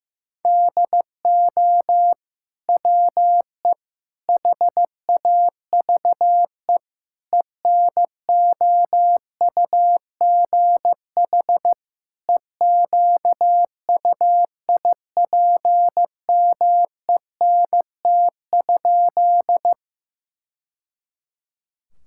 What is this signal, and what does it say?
Morse code, heard as a keyed tone, about 16 wpm, 700 Hz. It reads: DO WE HAVE ENOUGH EQUIPMENT?
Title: Sentences from Top 1000 Words - Review Part 5 - 15wpm